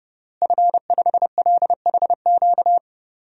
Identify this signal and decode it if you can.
Morse code, heard as a keyed tone, about 30 wpm, 700 Hz. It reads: F5LHQ
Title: DXCC-100 Call Signs - 5 Characters - Repeated 3x - 30wpm